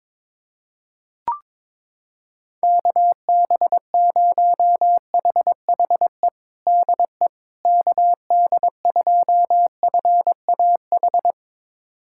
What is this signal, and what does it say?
Morse code, heard as a keyed tone, about 21 wpm, 700 Hz. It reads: KB0HHE DE KD2FAH
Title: USA Callsign DE Callsign 22wpm